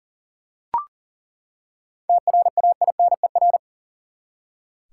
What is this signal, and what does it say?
Morse code, heard as a keyed tone, about 40 wpm, 700 Hz. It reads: TRAINER
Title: Seven Letter Words - 40wpm